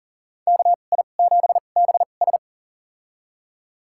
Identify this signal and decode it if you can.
Morse code, heard as a keyed tone, about 40 wpm, spaced as for 29 wpm, 700 Hz. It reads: KI7BS
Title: USA Call Signs - 5 Characters - 2x Character Spacing - 40wpm